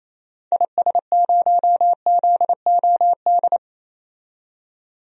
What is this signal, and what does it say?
Morse code, heard as a keyed tone, about 28 wpm, 700 Hz. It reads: IS0ZOD
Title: DXCC-100 Call Signs - 6 Characters - 28wpm